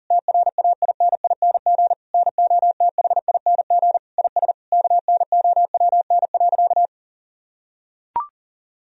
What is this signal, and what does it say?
Morse code, heard as a keyed tone, about 40 wpm, 700 Hz. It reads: TRAINING NOTHING IS KNOWN.
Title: Sentences from Top 1000 Words - Review Part 4 - 40wpm